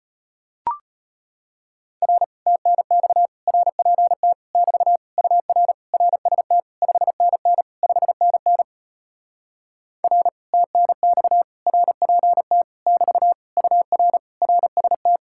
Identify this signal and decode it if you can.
Morse code, heard as a keyed tone, about 33 wpm, 700 Hz. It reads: R TNX RPT <BT> UR RST 5NN 5NN R TNX RPT <BT> UR RST